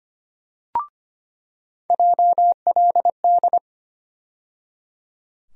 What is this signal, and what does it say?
Morse code, heard as a keyed tone, about 25 wpm, 700 Hz. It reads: JLD
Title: USA Call Sign Postfixes - 3 Letters 25wpm